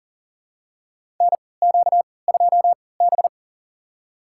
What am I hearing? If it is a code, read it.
Morse code, heard as a keyed tone, about 40 wpm, spaced as for 23 wpm, 700 Hz. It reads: NQ2B
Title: USA Call Signs - 4 Characters - 3x Character Spacing - 40wpm